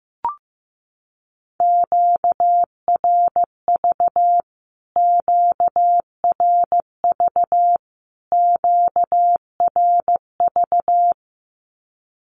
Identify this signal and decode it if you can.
Morse code, heard as a keyed tone, about 15 wpm, 700 Hz. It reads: QRV QRV QRV
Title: Q-Codes Repeated 3x 15wpm